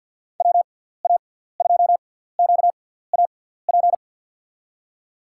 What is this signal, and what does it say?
Morse code, heard as a keyed tone, about 49 wpm, spaced as for 17 wpm, 700 Hz. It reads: WA2QAP
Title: USA Call Signs - 6 Characters - 6x Character Spacing - 50wpm